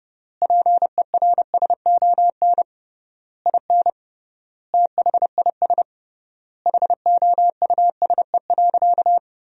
Morse code, heard as a keyed tone, about 30 wpm, 700 Hz. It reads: PERSON IN THIS HOUSE.
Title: 3x Word-Spacing - Sentences from Top 400 Words - 30wpm